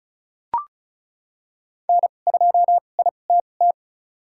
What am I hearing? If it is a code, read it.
Morse code, heard as a keyed tone, about 36 wpm, spaced as for 25 wpm, 700 Hz. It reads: N2ITT
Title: USA Call Signs - 5 Characters - 2x Character Spacing - 35wpm